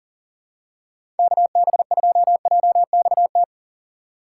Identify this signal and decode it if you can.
Morse code, heard as a keyed tone, about 40 wpm, 700 Hz. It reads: KB2JXT
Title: USA Call Signs - 6 Characters - 4x Character Spacing - 40wpm